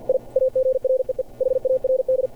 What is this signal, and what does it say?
Morse code, heard as a keyed tone, about 53 wpm, 520 Hz. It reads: I AGREE HARD
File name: archived / CRSnet